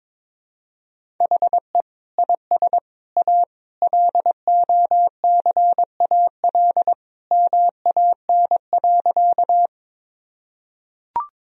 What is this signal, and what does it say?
Morse code, heard as a keyed tone, about 22 wpm, 700 Hz. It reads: HE IS A LOCAL MAN.
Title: Sentences from Top 1000 Words - Review Part 3 - 22wpm